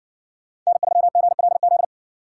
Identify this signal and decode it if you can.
Morse code, heard as a keyed tone, about 54 wpm, 700 Hz. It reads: N3GLZ